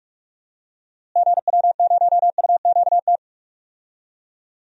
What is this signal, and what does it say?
Morse code, heard as a keyed tone, about 41 wpm, 700 Hz. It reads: GW0UQT
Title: DXCC-100 Call Signs - 6 Characters - Repeated 2x - 45wpm